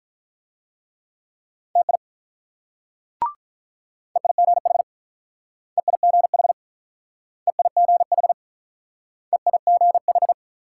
Morse code, heard as a keyed tone, about 48 wpm, 690 Hz. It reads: TI EIGH EIGH EIGH EIGH